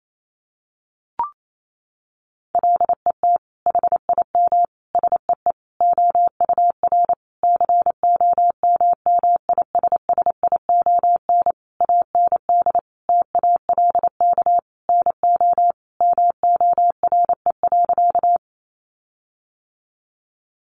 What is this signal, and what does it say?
Morse code, heard as a keyed tone, about 28 wpm, 700 Hz. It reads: LET HIM SEE OUR COMMISSION AND TALK NO MORE.
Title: Sentences from Top 1400 Words 28wpm